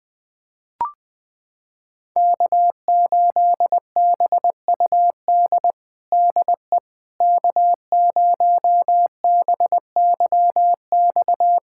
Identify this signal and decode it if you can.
Morse code, heard as a keyed tone, about 21 wpm, 700 Hz. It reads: K8BUD DE K0BYX
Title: USA Callsign DE Callsign - 5 Characters - 20wpm